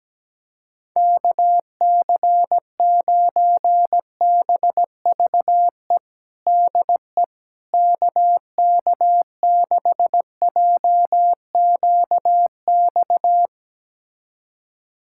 Morse code, heard as a keyed tone, about 17 wpm, 700 Hz. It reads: KC9BVE DE KK6JQX
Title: USA Callsign DE Callsign - 6 Characters - 17wpm